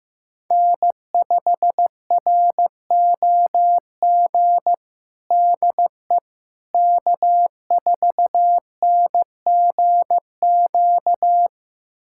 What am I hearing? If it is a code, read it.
Morse code, heard as a keyed tone, about 16 wpm, 700 Hz. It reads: N5ROG DE K4NGQ
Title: USA Callsign DE Callsign - 5 Characters - 15wpm